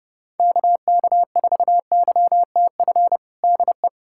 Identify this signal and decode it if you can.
Morse code, heard as a keyed tone, about 30 wpm, 700 Hz. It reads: KK4YTF DE
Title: USA Callsign DE Callsign - 6 Characters - 30wpm